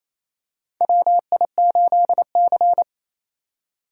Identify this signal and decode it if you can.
Morse code, heard as a keyed tone, about 28 wpm, 700 Hz. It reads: WI8C
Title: USA Call Signs - 4 Characters - 3x Character Spacing - 28wpm